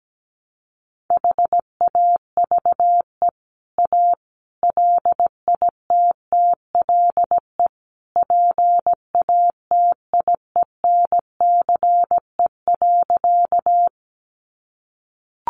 Morse code, heard as a keyed tone, about 17 wpm, 700 Hz. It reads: HAVE A LITTLE PATIENCE.
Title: Sentences from Top 1300 Words 17wpm